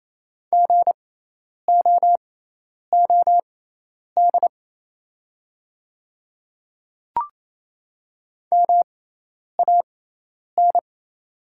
Morse code, heard as a keyed tone, about 28 wpm, spaced as for 10 wpm, 700 Hz. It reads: GOOD MAN
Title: 6x Character Spacing - Top 300 Words - 28wpm